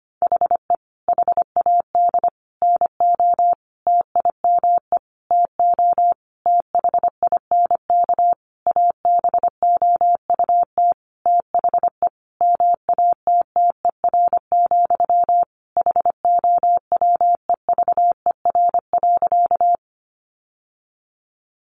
Morse code, heard as a keyed tone, about 25 wpm, 700 Hz. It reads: HE HAD NO TIME TO THINK ABOUT THE MATTER, HOWEVER.
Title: Sentences from Top 700 Words 25wpm